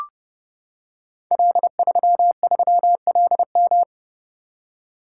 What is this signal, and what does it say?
Morse code, heard as a keyed tone, about 30 wpm, 700 Hz. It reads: L33LM
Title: Code Groups - 5 Letter:Numbers - 30wpm